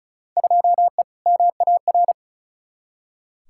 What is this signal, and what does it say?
Morse code, heard as a keyed tone, about 36 wpm, 690 Hz. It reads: 2E MAR
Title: Field Day Exchanges Encore - 35wpm